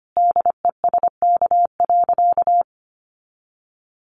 Morse code, heard as a keyed tone, about 25 wpm, 700 Hz. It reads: DESK.
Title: Additional Easy Phrases 25wpm